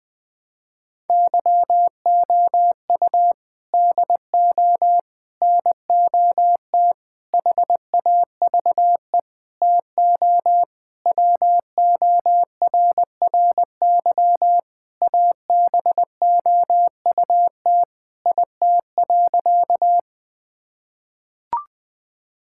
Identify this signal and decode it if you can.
Morse code, heard as a keyed tone, about 20 wpm, 700 Hz. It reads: YOU DO NOT HAVE TO WORRY ABOUT IT.
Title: Sentences with 5-Word Phrases - 20wpm